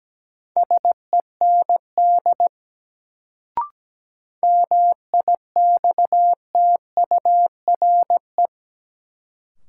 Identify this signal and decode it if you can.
Morse code, heard as a keyed tone, about 17 wpm, 700 Hz. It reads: SEND MIXTURE
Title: Sentences from Top 1500 Words - New Word Drill - 17wpm